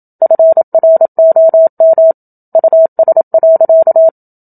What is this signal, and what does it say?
Morse code, heard as a keyed tone, about 27 wpm, 650 Hz. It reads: FROM US.
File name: Book Two / 01-Chapter 1 - 27 WPM